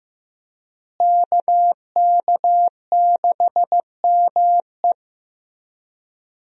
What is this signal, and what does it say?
Morse code, heard as a keyed tone, about 15 wpm, 700 Hz. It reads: KK6ME
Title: USA Call Signs - 5 Characters - Repeated 2x - 15wpm